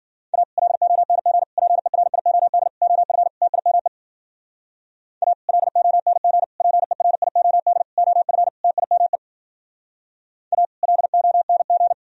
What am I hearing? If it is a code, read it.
Morse code, heard as a keyed tone, about 52 wpm, 700 Hz. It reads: A LONG PERIOD OF TIME A LONG PERIOD OF TIME A LONG